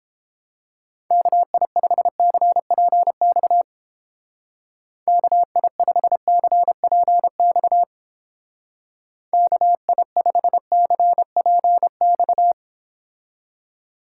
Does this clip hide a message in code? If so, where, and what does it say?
Yes: Morse code from 1.1 to 12.5 s, heard as a keyed tone, about 29 wpm, 700 Hz. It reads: KI5CPX KI5CPX KI5CPX